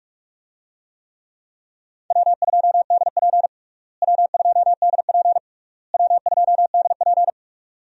Morse code, heard as a keyed tone, about 46 wpm, 700 Hz. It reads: W2DP W2DP W2DP
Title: USA Call Signs - 4 Characters - Repeated 3x - 45wpm